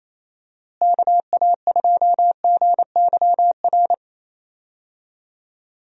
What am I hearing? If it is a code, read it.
Morse code, heard as a keyed tone, about 28 wpm, 700 Hz. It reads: KA2GYR
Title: USA Call Signs - 6 Characters - Repeated 3x - 28wpm